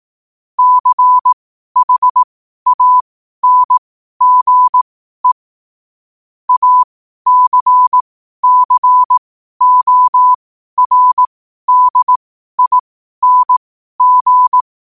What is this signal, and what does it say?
Morse code, heard as a keyed tone, about 18 wpm, spaced as for 12 wpm, 1000 Hz. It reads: CHANGE ACCORDING
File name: qotdfiles / 12wpm